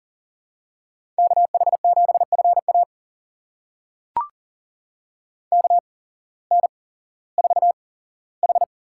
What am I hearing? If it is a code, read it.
Morse code, heard as a keyed tone, about 38 wpm, spaced as for 18 wpm, 700 Hz. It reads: KH7FA KN4H
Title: USA Call Signs - 8x Character Spacing - 40wpm